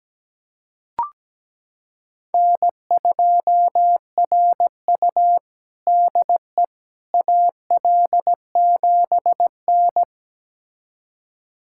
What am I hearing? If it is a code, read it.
Morse code, heard as a keyed tone, about 17 wpm, 700 Hz. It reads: N2RU DE AL7N
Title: USA Callsign DE Callsign - 4 Characters - 17wpm